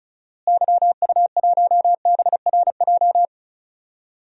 Morse code, heard as a keyed tone, about 34 wpm, 700 Hz. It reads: YU1BRJ
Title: DXCC-100 Call Signs - 6 Characters - Repeated 2x - 35wpm